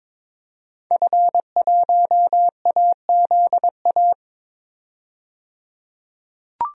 Morse code, heard as a keyed tone, about 22 wpm, 700 Hz. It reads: F1AZA
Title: DXCC-100 Call Signs - 5 Characters - 22wpm